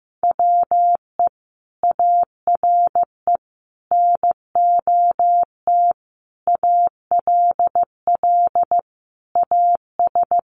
Morse code, heard as a keyed tone, about 15 wpm, 700 Hz. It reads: WE ARE NOT ALL AS
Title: Sentences from Top 1200 Words 15wpm